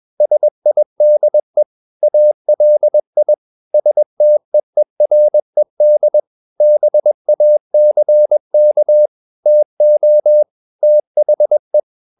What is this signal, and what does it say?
Morse code, heard as a keyed tone, about 21 wpm, 600 Hz. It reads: SIDE ALI STEERED BACK TO THE
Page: Romanen Never af Ken Follet på engelsk.